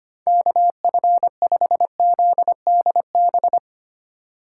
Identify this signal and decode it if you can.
Morse code, heard as a keyed tone, about 31 wpm, 700 Hz. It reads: KF5ZDB
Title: Speed Racing - USA 2x3 Call Signs 25wpm